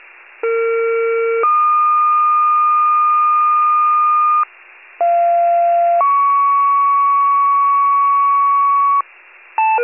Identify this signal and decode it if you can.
Two-tone page, heard as a one-second tone, then a three-second tone, about 460 then 1220 Hz, about 690 then 1120 Hz. It is heard from 0.4 to 9.0 s.
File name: pager / 41 RESCUE / wcalert